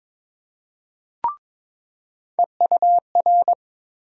Morse code, heard as a keyed tone, about 22 wpm, 700 Hz. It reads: EUR
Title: USA Call Sign Postfixes - 3 Letters 22wpm